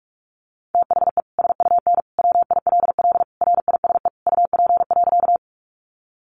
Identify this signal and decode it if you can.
Morse code, heard as a keyed tone, about 44 wpm, 700 Hz. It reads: THE SUN WILL RISE UP.
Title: Sentences from Top 900 Words 45wpm